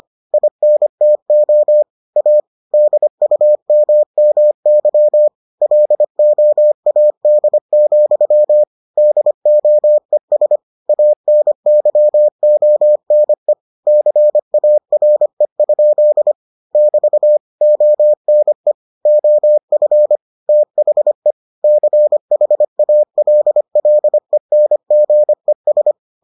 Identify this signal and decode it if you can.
Morse code, heard as a keyed tone, about 25 wpm, 600 Hz. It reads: INTO A DUMMY LOAD, DOES ANYONE CARE? <BT> ONE OF THE CHALLENGES